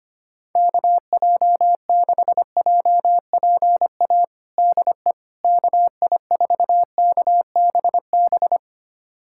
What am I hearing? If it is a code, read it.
Morse code, heard as a keyed tone, about 25 wpm, 700 Hz. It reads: KJ6JPA DE KI4KBB
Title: USA Callsign DE Callsign - 6 Characters - 25wpm